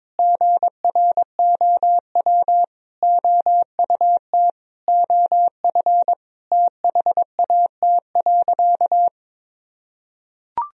Morse code, heard as a keyed tone, about 22 wpm, 700 Hz. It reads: GROW OUT OF THAT.
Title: Sentences from Top 1000 Words - Review Part 1 - 22wpm